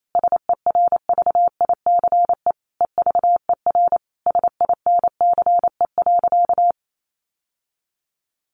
Morse code, heard as a keyed tone, about 28 wpm, 700 Hz. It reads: SERVICE EVER SINCE.
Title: Sentences from Top 700 Words 28wpm